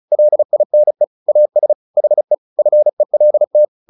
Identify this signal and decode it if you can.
Morse code, heard as a keyed tone, about 34 wpm, 600 Hz. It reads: LINE AS HE FELT
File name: the old man and the sea 2 35wpm